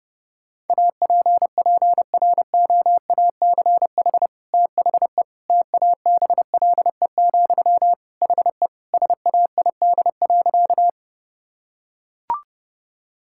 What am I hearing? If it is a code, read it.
Morse code, heard as a keyed tone, about 30 wpm, 700 Hz. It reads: APPROACH THE TABLE, HE SAID.
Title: Sentences from Top 1000 Words - Review Part 6 - 30wpm